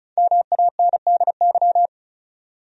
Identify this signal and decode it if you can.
Morse code, heard as a keyed tone, about 36 wpm, 700 Hz. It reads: MANDY
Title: USA Common Names - Max 5 Letters - 35wpm